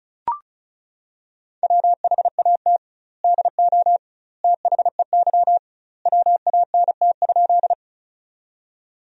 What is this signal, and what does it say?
Morse code, heard as a keyed tone, about 34 wpm, 700 Hz. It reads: WHAT DO THEY WANT?
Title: 2x Word-Spacing - Sentences from Top 200 Words - 35wpm